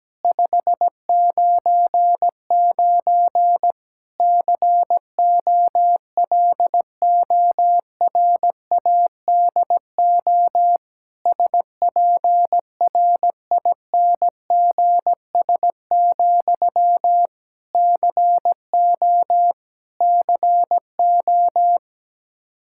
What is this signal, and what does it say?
Morse code, heard as a keyed tone, about 17 wpm, 700 Hz. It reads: 599 COLORADO SPRINGS, CO CO
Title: K1USN SST QSO Practice 17wpm